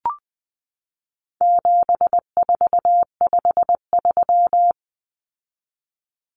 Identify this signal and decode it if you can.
Morse code, heard as a keyed tone, about 20 wpm, 700 Hz. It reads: 7453
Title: Numbers - 4-Digit 20wpm